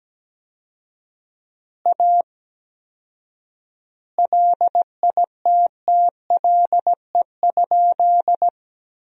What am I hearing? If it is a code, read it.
Morse code, heard as a keyed tone, about 17 wpm, 700 Hz. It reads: A LITTLE?
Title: 4x Word-Spacing - Sentences from Top 500 Words - 17wpm